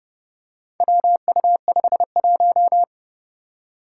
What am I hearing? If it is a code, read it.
Morse code, heard as a keyed tone, about 31 wpm, 700 Hz. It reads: WU51
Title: Four Challenging Letter-Number Combinations 30wpm